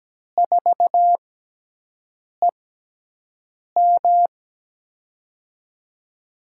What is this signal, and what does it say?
Morse code, heard as a keyed tone, about 17 wpm, spaced as for 6 wpm, 700 Hz. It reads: 4EM
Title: USA Call Signs - 6x Character Spacing - 17wpm